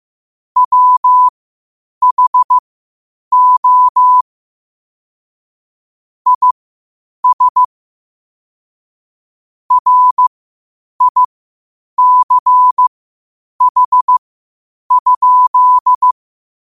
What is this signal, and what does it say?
Morse code, heard as a keyed tone, about 15 wpm, spaced as for 8 wpm, 1000 Hz. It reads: WHO IS RICH?
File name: qotdfiles / 8wpm